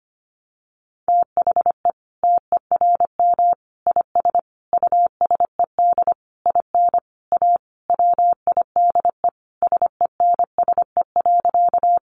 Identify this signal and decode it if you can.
Morse code, heard as a keyed tone, about 25 wpm, 700 Hz. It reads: THE TERM IS USED IN A WIDE SENSE.
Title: Sentences from Top 900 Words 25wpm